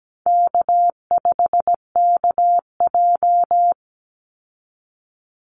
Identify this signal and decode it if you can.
Morse code, heard as a keyed tone, about 17 wpm, 700 Hz. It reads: K5KJ
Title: Call Signs Repeated 3x - 17wpm